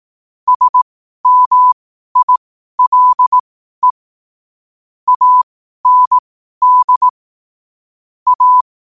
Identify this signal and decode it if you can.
Morse code, heard as a keyed tone, about 18 wpm, spaced as for 12 wpm, 1000 Hz. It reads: SMILE AND A
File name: qotdfiles / 12wpm